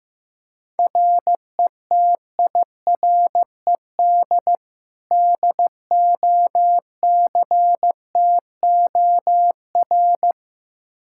Morse code, heard as a keyed tone, about 16 wpm, 700 Hz. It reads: RETIRED DOCTOR
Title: Practice QSO - Exchange 3 - More Info - 15wpm